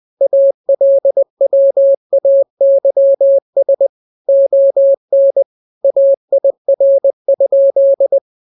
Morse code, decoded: ALWAYS ON AIR?